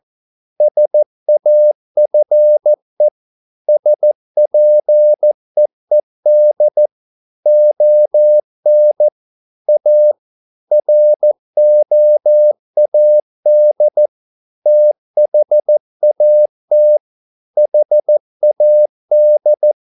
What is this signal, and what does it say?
Morse code, heard as a keyed tone, about 14 wpm, 600 Hz. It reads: SAFE SPEED ON A ROAD THAT HAD